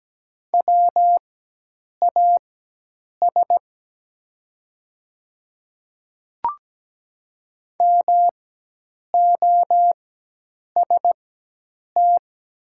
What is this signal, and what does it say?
Morse code, heard as a keyed tone, about 17 wpm, spaced as for 8 wpm, 700 Hz. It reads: WAS MOST
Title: 4x Character Spacing - Top 100 Words - 17wpm